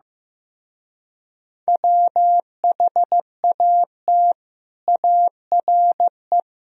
Morse code, heard as a keyed tone, about 15 wpm, 700 Hz. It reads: WHAT ARE
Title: Speed-Racing - Sets of 4 Words 15wpm